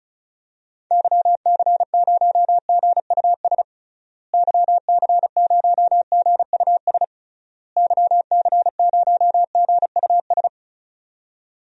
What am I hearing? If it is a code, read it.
Morse code, heard as a keyed tone, about 36 wpm, 700 Hz. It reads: YC0GUS YC0GUS YC0GUS
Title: DXCC-100 Call Signs - 6 Characters - Repeated 3x - 35wpm